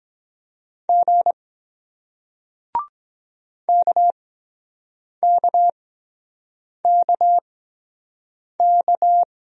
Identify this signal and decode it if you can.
Morse code, heard as a keyed tone, about 24 wpm, spaced as for 7 wpm, 700 Hz. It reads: G KKKK